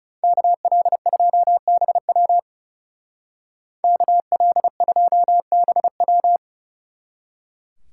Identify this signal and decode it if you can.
Morse code, heard as a keyed tone, about 35 wpm, 700 Hz. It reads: KL2BW KL2BW
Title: Speed Racing - USA 2x2 Call Signs 30wpm